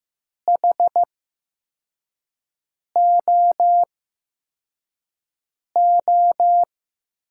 Morse code, heard as a keyed tone, about 15 wpm, spaced as for 4 wpm, 700 Hz. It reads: HOO
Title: USA Call Signs - 6 Characters - 8x Character Spacing - 15wpm